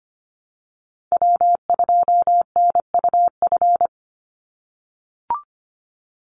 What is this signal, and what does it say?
Morse code, heard as a keyed tone, about 25 wpm, 700 Hz. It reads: W2NUF
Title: USA Call Signs 25wpm